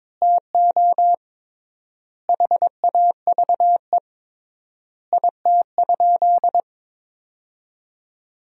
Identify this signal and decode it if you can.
Morse code, heard as a keyed tone, about 21 wpm, 700 Hz. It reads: TO HAVE IT?
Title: 3x Word-Spacing - Sentences from Top 100 Words - 22wpm